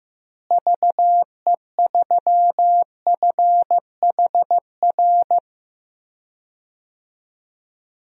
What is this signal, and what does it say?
Morse code, heard as a keyed tone, about 15 wpm, 700 Hz. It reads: VE3FHR
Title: DXCC-100 Call Signs - 6 Characters - Repeated 2x - 15wpm